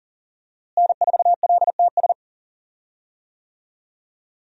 Morse code, heard as a keyed tone, about 40 wpm, 700 Hz. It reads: N4LTS
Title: USA Call Signs - 5 Characters - Repeated 2x - 40wpm